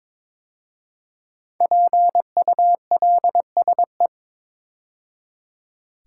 Morse code, heard as a keyed tone, about 22 wpm, 700 Hz. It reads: PULSE
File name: 5 Letter Words - First 3 Letters Given - 22wpm